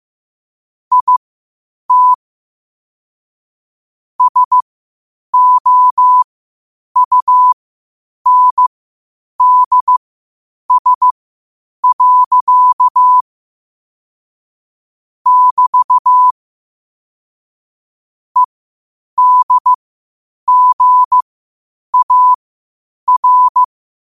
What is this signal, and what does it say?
Morse code, heard as a keyed tone, about 15 wpm, spaced as for 8 wpm, 1000 Hz. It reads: IT SOUNDS. <BT> EDGAR
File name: qotdfiles / 8wpm